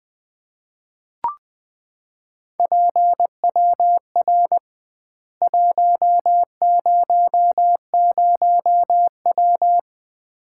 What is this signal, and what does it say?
Morse code, heard as a keyed tone, about 20 wpm, 700 Hz. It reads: PWR 100W
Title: Common QSO Phrases - 2x Word-Spacing - 20wpm